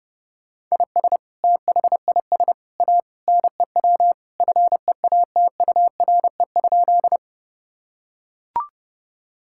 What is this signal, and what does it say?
Morse code, heard as a keyed tone, about 30 wpm, 700 Hz. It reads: IS THIS A NEW FEATURE?
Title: Sentences from Top 1000 Words - Review Part 5 - 30wpm